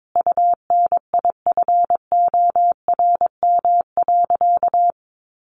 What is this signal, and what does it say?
Morse code, heard as a keyed tone, about 22 wpm, 700 Hz. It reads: UNIFORM.
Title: Easy Phrases 22wpm